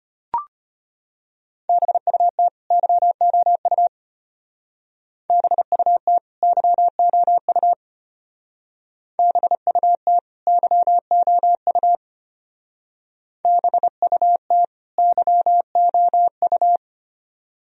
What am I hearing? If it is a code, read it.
Morse code, heard as a keyed tone, about 30 wpm, 700 Hz. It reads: BUT YOU BUT YOU BUT YOU BUT YOU
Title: Speed-Racing - Sets of 2 Words 25wpm